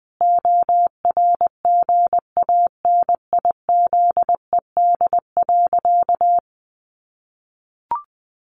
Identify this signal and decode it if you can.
Morse code, heard as a keyed tone, about 20 wpm, 700 Hz. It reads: ORGANIZED.
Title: Sentences from Top 1300 Words 20wpm